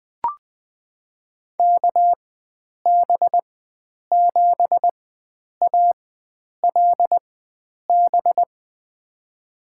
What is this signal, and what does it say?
Morse code, heard as a keyed tone, about 20 wpm, spaced as for 9 wpm, 700 Hz. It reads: KB7ALB
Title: USA Call Signs - 6 Characters - 4x Character Spacing - 20wpm